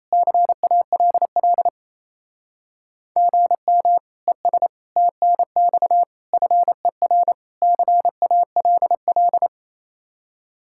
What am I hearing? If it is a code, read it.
Morse code, heard as a keyed tone, about 35 wpm, 700 Hz. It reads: CALL GM ES TNX FER CALL